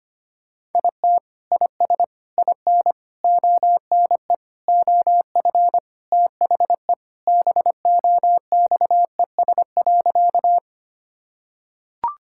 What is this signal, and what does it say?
Morse code, heard as a keyed tone, about 25 wpm, 700 Hz. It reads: IT IS IN ONE OF THE BOXES.
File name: practice / 25wpm / Sentences with 3-Word Phrases - 25wpm